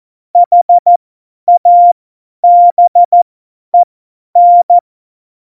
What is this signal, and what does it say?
Morse code, heard as a keyed tone, about 14 wpm, spaced as for 10 wpm, 700 Hz. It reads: HABEN
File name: Morsekurs / Witze / Witz 07. Januar 2020, 50 F70